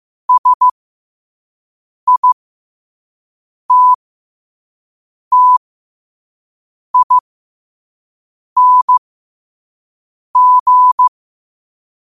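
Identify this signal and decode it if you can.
Morse code, heard as a keyed tone, about 15 wpm, spaced as for 5 wpm, 1000 Hz. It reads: SITTING
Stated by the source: Morse code sent at 5 wpm.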